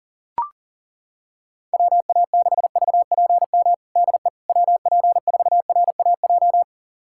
Morse code, heard as a keyed tone, about 40 wpm, 700 Hz. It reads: WA6VPM DE WP4RAJ
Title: USA Callsign DE Callsign - 6 Characters - 40wpm